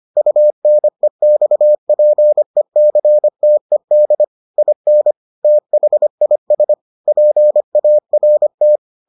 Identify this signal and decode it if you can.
Morse code, heard as a keyed tone, about 25 wpm, 600 Hz. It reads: UNEXPECTED IN THIS PART